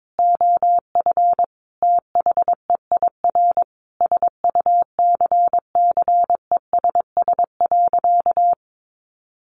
Morse code, heard as a keyed tone, about 22 wpm, 700 Hz. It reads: OF THEIR SUCCESS.
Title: Sentences from Top 900 Words 22wpm